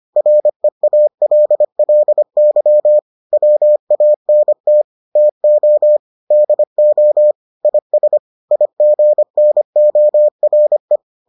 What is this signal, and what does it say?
Morse code, heard as a keyed tone, about 25 wpm, 600 Hz. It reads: REALLY WANT TO DO IS IGNORE